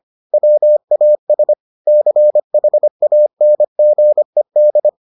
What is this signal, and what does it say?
Morse code, heard as a keyed tone, about 25 wpm, 600 Hz. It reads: WAS CHANGED